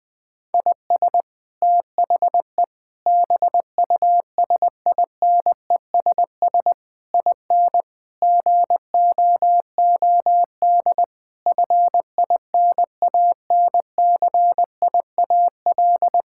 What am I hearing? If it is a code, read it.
Morse code, heard as a keyed tone, about 21 wpm, 700 Hz. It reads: IS THE BUSINESS IN GOOD FINANCIAL
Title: Sentences from Top 1000 Words - Review Part 5 - 20wpm